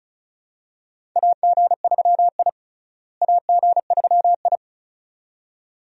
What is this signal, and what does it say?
Morse code, heard as a keyed tone, about 34 wpm, 700 Hz. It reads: AG3I AG3I
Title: USA Call Signs - 4 Characters - Repeated 2x - 35wpm